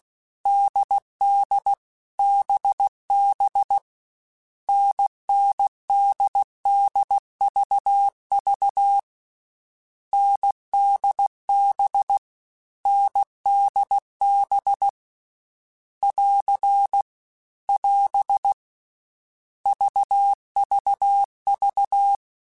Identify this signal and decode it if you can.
Morse code, heard as a keyed tone, about 16 wpm, 800 Hz. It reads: DD BB NNDDVV NDB NDB <AR> <AS> VVV